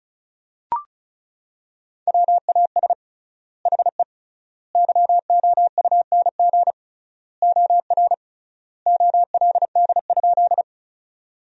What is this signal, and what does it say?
Morse code, heard as a keyed tone, about 34 wpm, 700 Hz. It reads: WAS HE YOUNG OR OLD?
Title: 3x Word-Spacing - Sentences from Top 400 Words - 35wpm